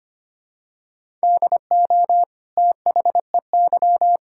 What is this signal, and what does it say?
Morse code, heard as a keyed tone, about 25 wpm, 700 Hz. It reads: DO THEY
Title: Sentences from Top 1000 Words - Review Part 6 - 25wpm